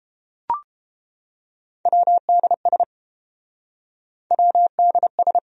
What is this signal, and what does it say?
Morse code, heard as a keyed tone, about 30 wpm, 700 Hz. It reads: WDS WDS